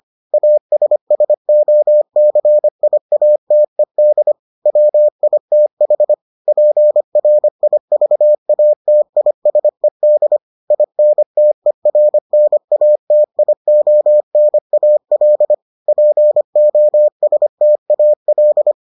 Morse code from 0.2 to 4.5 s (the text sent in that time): ASSOCIATED